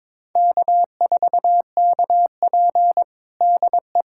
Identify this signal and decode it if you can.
Morse code, heard as a keyed tone, about 22 wpm, 700 Hz. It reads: K4KP DE
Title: USA Callsign DE Callsign 22wpm